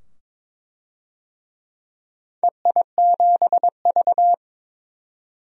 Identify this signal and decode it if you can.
Morse code, heard as a keyed tone, about 21 wpm, 700 Hz. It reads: EI7V
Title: DXCC-100 Call Signs - 4 Characters - 22wpm